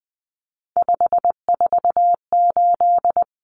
Morse code, heard as a keyed tone, about 20 wpm, 700 Hz. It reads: 548
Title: Numbers - 3-Digit 20wpm